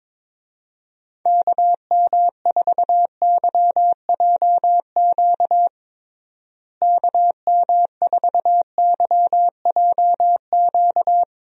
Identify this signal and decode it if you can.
Morse code, heard as a keyed tone, about 22 wpm, 700 Hz. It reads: KM4YJQ KM4YJQ
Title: USA Call Signs - 6 Characters - Repeated 3x - 22wpm